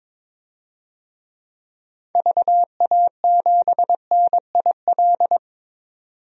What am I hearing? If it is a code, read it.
Morse code, heard as a keyed tone, about 22 wpm, 700 Hz. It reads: VA7NIL
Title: DXCC-100 Call Signs - 6 Characters - 22wpm